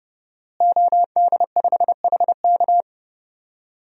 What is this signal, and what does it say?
Morse code, heard as a keyed tone, about 30 wpm, 700 Hz. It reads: OD5HK
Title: DXCC-100 Call Signs - 5 Characters - Repeated 3x - 30wpm